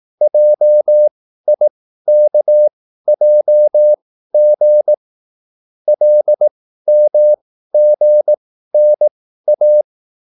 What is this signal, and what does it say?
Morse code, heard as a keyed tone, about 18 wpm, spaced as for 13 wpm, 600 Hz. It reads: JIKJG LMGNA
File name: QSL / CW lesson